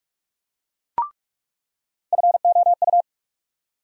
Morse code, heard as a keyed tone, about 43 wpm, 700 Hz. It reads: FOU
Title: USA Call Sign Postfixes - 3 Letters 45wpm